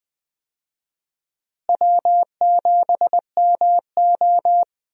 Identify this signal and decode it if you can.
Morse code, heard as a keyed tone, about 20 wpm, 700 Hz. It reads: W7MO
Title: USA Call Signs - 4 Characters - Repeated 3x - 20wpm